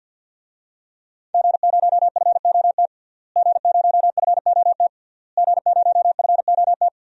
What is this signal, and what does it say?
Morse code, heard as a keyed tone, about 52 wpm, 700 Hz. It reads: G0FOT G0FOT G0FOT
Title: DXCC-100 Call Signs - 5 Characters - Repeated 3x - 50wpm